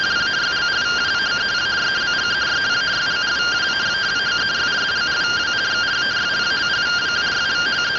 Two-tone teleprinter signal, about 170 Hz shift.
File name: findings / CT AXIAL -142.gen-02 rtty rx